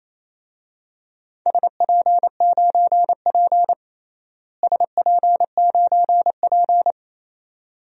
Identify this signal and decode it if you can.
Morse code, heard as a keyed tone, about 28 wpm, 700 Hz. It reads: SP9P SP9P
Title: DXCC-100 Call Signs - 4 Characters - Repeated 2x - 28wpm